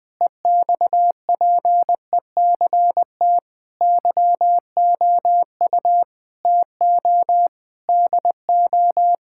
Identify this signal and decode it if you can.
Morse code, heard as a keyed tone, about 20 wpm, 700 Hz. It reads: EXPECT YOU TO DO
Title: Sentences with 3-Word Phrases - Spoken Beginning - 20wpm